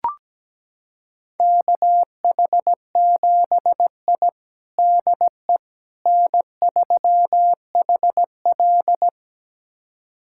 Morse code, heard as a keyed tone, about 17 wpm, 700 Hz. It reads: KH7I DE N3HL
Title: USA Callsign DE Callsign - 4 Characters - 17wpm